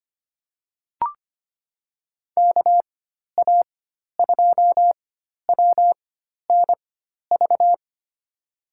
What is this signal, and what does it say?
Morse code, heard as a keyed tone, about 25 wpm, spaced as for 12 wpm, 700 Hz. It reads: KA2WNV
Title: USA Call Signs - 6 Characters - 4x Character Spacing - 25wpm